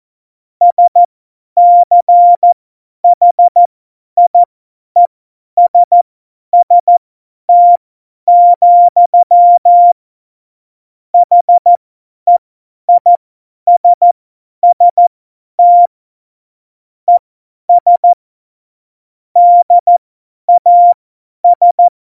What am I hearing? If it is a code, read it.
Morse code, heard as a keyed tone, about 14 wpm, spaced as for 10 wpm, 700 Hz. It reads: SCHIESST, HEISST ES DAS
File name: Morsekurs / Witze / Witz 21. Januar 2020, 50 F70